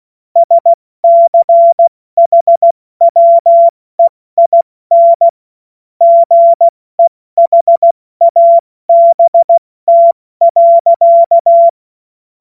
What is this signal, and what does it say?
Morse code, heard as a keyed tone, about 16 wpm, spaced as for 14 wpm, 680 Hz. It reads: SCHWEIN GEHABT.
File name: Witz-05-August-2025-70-F80